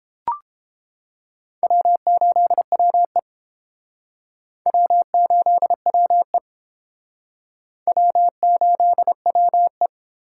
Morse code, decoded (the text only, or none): W8WE W8WE W8WE